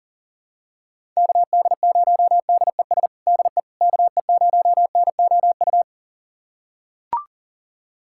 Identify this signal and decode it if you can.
Morse code, heard as a keyed tone, about 37 wpm, 700 Hz. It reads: KD0DES DE KE0NOU
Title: USA Callsign DE Callsign - 6 Characters - 40wpm